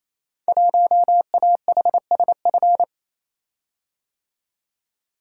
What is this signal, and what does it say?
Morse code, heard as a keyed tone, about 28 wpm, 700 Hz. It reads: 1AHSF
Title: Code Groups - 5 Letter:Numbers - 28wpm